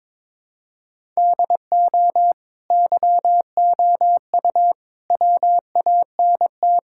Morse code, heard as a keyed tone, about 22 wpm, 700 Hz. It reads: DO YOU WANT
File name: practice / 22wpm / Sentences with 4-Word Phrases - 22wpm